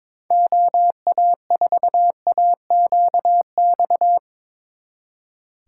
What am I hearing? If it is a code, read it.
Morse code, heard as a keyed tone, about 24 wpm, 700 Hz. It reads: OA4AQX
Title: Speed Racing - USA 2x3 Call Signs 22wpm